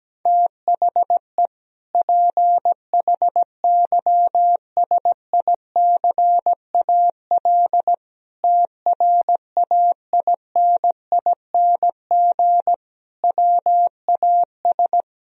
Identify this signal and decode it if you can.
Morse code, heard as a keyed tone, about 17 wpm, 700 Hz. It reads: THE PHYSICAL TRAINING WAS
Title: Sentences from Top 1000 Words - Review Part 4 - 17wpm